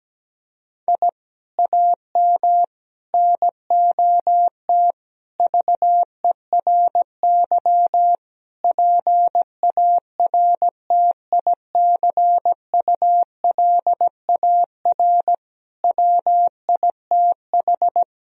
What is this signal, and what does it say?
Morse code, heard as a keyed tone, about 17 wpm, 700 Hz. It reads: I AM NOT VERY PARTICULAR WITH
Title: Sentences from Top 1000 Words - Review Part 4 - 17wpm